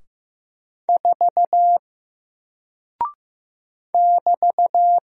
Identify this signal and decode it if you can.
Morse code, heard as a keyed tone, about 15 wpm, 700 Hz. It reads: 4 <BT>
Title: Single Character (Letter:Number:Punctuation:Prosign) Alt 15wpm